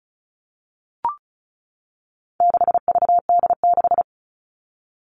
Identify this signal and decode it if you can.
Morse code, heard as a keyed tone, about 34 wpm, 700 Hz. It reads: 6VD6
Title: All Cut Numbers - 4-Digit 35wpm